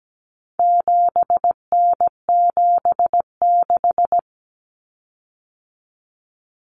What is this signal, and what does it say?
Morse code, heard as a keyed tone, about 17 wpm, 700 Hz. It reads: 7N76
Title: Common Cut Numbers - 4-Digit 17wpm